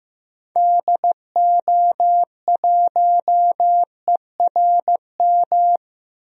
Morse code, heard as a keyed tone, about 15 wpm, 700 Hz. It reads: DO1ERM
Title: DXCC-100 Call Signs - 6 Characters - Repeated 3x - 15wpm